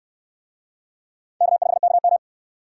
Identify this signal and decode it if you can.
Morse code, heard as a keyed tone, about 69 wpm, 700 Hz. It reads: K5KM